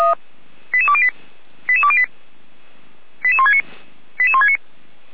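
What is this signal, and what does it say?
Five-tone selective call (ZVEI), heard as quick runs of five tones, tones of about 70 ms. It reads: 88208 88208 88169 88169
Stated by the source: ZVEI five-tone selective call received by SDR, feuerwehr-orlamuende.org (Orlamünde fire brigade) server.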